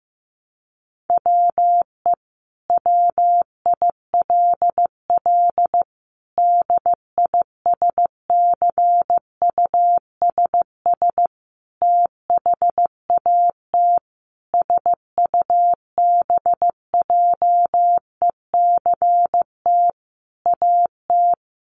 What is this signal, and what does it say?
Morse code, heard as a keyed tone, about 15 wpm, 700 Hz. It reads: WE WILL DISCUSS THAT SUBJECT AT
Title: Sentences from Top 1000 Words 15wpm